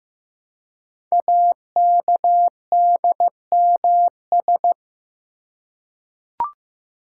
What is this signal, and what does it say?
Morse code, heard as a keyed tone, about 15 wpm, 700 Hz. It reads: AKDMS
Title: Code Groups - 5 Letters - 15wpm